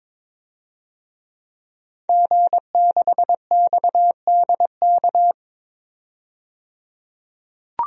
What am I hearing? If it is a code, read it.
Morse code, heard as a keyed tone, about 22 wpm, 700 Hz. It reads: G6XDK
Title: DXCC-100 Call Signs - 5 Characters - 22wpm